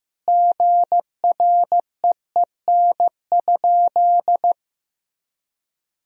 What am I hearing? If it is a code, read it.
Morse code, heard as a keyed tone, about 15 wpm, 700 Hz. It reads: GREEN?
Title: Sentences from Top 1000 Words - Review Part 2 - 15wpm